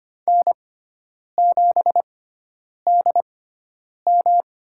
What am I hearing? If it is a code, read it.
Morse code, heard as a keyed tone, about 25 wpm, spaced as for 9 wpm, 700 Hz. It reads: N7DM
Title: USA Call Signs - 6x Character Spacing - 25wpm